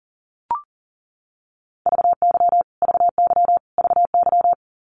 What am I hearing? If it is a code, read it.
Morse code, heard as a keyed tone, about 40 wpm, 700 Hz. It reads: VY VY VY
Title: Abbreviations Repeated 3x - 40wpm